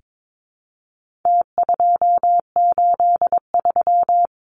Morse code, heard as a keyed tone, about 22 wpm, 700 Hz. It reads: T283
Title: Numbers - Contest Style - 4-Digit 22wpm